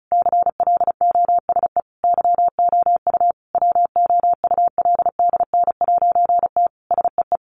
Morse code, heard as a keyed tone, about 34 wpm, 700 Hz. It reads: CLOSE YOU WOULDN'T SEE
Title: A Sound Of Thunder 35wpm